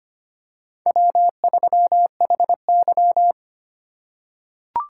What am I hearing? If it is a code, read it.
Morse code, heard as a keyed tone, about 25 wpm, 700 Hz. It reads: W3HY
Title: USA Call Signs - 4x Character Spacing - 25wpm